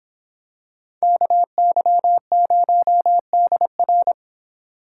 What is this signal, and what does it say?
Morse code, heard as a keyed tone, about 26 wpm, 700 Hz. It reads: KY0DR